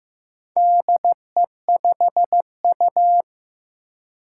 Morse code, heard as a keyed tone, about 15 wpm, 700 Hz. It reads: DE5U
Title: Four Challenging Letter-Number Combinations 15wpm